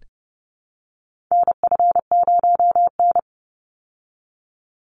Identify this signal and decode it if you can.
Morse code, heard as a keyed tone, about 30 wpm, 700 Hz. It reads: NF0N
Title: Call Signs Repeated 3x - 30wpm